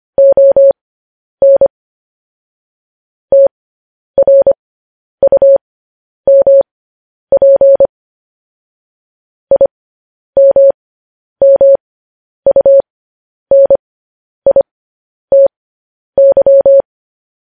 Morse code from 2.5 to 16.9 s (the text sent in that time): TRUMP IMMUNITY